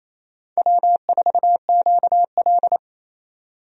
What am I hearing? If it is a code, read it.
Morse code, heard as a keyed tone, about 28 wpm, 700 Hz. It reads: W4QL
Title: USA Call Signs - 4 Characters - 28wpm